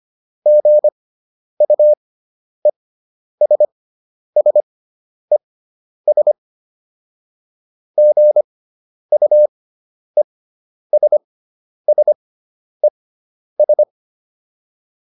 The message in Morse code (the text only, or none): GUESSES GUESSES